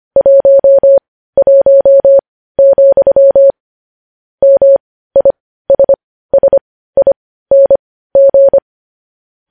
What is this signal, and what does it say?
Morse code, heard as a keyed tone, about 25 wpm, spaced as for 15 wpm, 550 Hz. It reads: 11, MISSING
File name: podcast / News headlines-1x-25x15